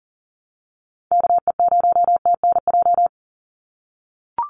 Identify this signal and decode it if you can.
Morse code, heard as a keyed tone, about 40 wpm, 700 Hz. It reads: KE0TNJ
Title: USA Call Signs 40wpm